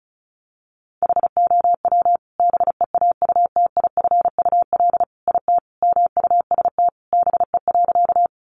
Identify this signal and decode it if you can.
Morse code, heard as a keyed tone, about 34 wpm, 700 Hz. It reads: HOW BEAUTIFUL IT MUST BE.
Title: Sentences from Top 900 Words 35wpm